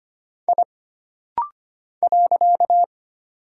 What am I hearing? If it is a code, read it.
Morse code, heard as a keyed tone, about 25 wpm, 700 Hz. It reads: I .
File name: Single Character (Letter:Number:Punctuation:Prosign) - Rapid-Fire - 25wpm